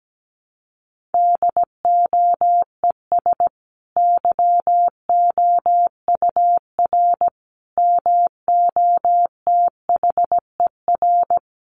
Morse code, heard as a keyed tone, about 17 wpm, 700 Hz. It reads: DOES YOUR MOTHER
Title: Sentences from Top 1400 Words 17wpm